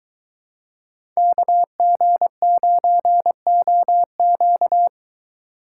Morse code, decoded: KG9OQ

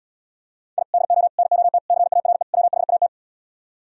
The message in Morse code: I DO NOT CARE WHAT

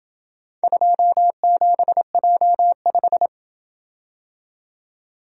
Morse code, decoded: 27J5

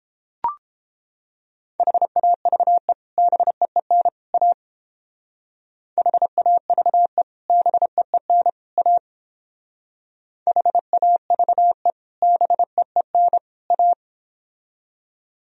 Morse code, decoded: HAVE BEEN A HAVE BEEN A HAVE BEEN A

